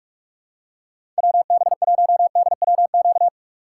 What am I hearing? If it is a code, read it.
Morse code, heard as a keyed tone, about 46 wpm, 700 Hz. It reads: WB1DWQ
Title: USA Call Signs - 6 Characters - 3x Character Spacing - 45wpm